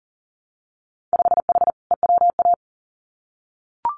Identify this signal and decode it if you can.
Morse code, heard as a keyed tone, about 39 wpm, 700 Hz. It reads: 5H EWA
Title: Winter Field Day Exchanges Encore 40wpm